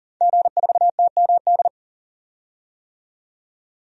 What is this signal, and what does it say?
Morse code, heard as a keyed tone, about 40 wpm, 700 Hz. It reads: G4TMD
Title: DXCC-100 Call Signs - 5 Characters - 40wpm